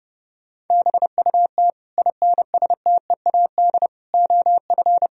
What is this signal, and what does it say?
Morse code, heard as a keyed tone, about 30 wpm, 700 Hz. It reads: BUT INSTEAD OF
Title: Sentences from Top 1000 Words - Review Part 4 - 30wpm